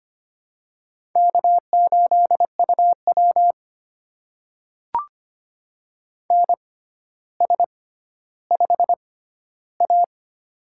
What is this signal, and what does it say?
Morse code, heard as a keyed tone, about 25 wpm, spaced as for 13 wpm, 700 Hz. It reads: K8UW NS5A
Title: USA Call Signs - 4 Characters - 6x Character Spacing - 25wpm